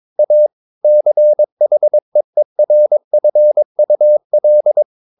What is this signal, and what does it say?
Morse code, heard as a keyed tone, about 22 wpm, 600 Hz. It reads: A CHEERFUL
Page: Romanen Never af Ken Follet på engelsk.